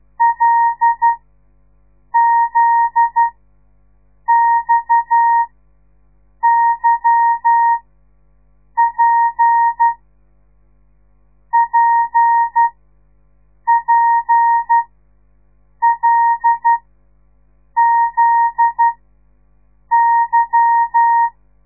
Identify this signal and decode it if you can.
Morse code, heard as a keyed tone, about 12 wpm, spaced as for 6 wpm, 930 Hz. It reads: LZXYP PPLZY
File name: radiomorningstar / morse / les 8